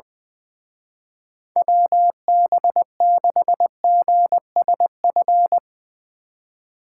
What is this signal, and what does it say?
Morse code, heard as a keyed tone, about 20 wpm, 700 Hz. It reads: WB6GSF
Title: Speed Racing - USA Call Signs 20wpm